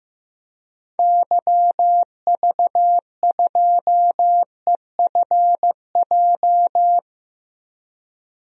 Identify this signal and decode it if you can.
Morse code, heard as a keyed tone, about 15 wpm, 700 Hz. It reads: YV2EFJ
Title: DXCC-100 Call Signs - 6 Characters - Repeated 2x - 15wpm